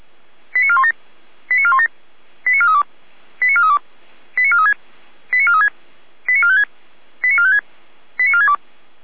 Five-tone selective call (ZVEI), heard as quick runs of five tones, tones of about 70 ms. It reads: 89417 89417 89432 89432 89436 89436 89456 89456 89462